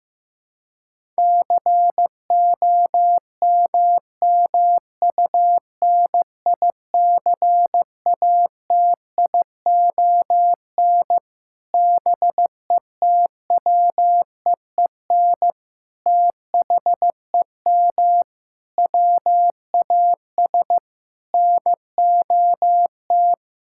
Morse code, heard as a keyed tone, about 15 wpm, 700 Hz. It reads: COMMUNICATION BETWEEN THEM WAS NOT
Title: Sentences from Top 1000 Words - Review Part 6 - 15wpm